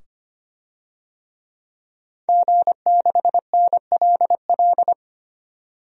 Morse code, heard as a keyed tone, about 25 wpm, 700 Hz. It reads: G6NLL
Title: DXCC-100 Call Signs - 5 Characters - 25wpm